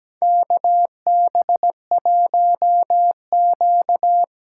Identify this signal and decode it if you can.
Morse code, heard as a keyed tone, about 17 wpm, 700 Hz. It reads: KB1Q
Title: USA Call Signs - 6 Characters - 8x Character Spacing - 17wpm